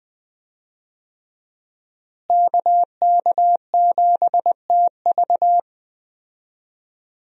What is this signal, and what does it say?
Morse code, heard as a keyed tone, about 21 wpm, 700 Hz. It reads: KK7TV
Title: USA Call Signs - 5 Characters - Repeated 3x - 20wpm